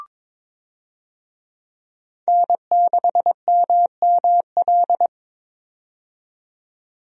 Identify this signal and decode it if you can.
Morse code, heard as a keyed tone, about 22 wpm, 700 Hz. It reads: N6MML
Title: USA Call Signs - 5 Characters - 22wpm